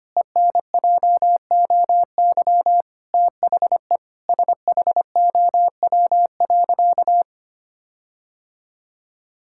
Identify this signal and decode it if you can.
Morse code, heard as a keyed tone, about 25 wpm, 700 Hz. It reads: ENJOY THE SHOW.
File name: practice / 25wpm / Sentences with 4-Word Phrases - 25wpm